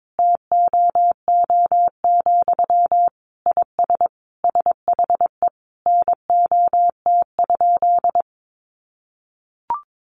Morse code, heard as a keyed tone, about 22 wpm, 700 Hz. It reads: TOO, IS SHE NOT?
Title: Sentences from Top 1300 Words 22wpm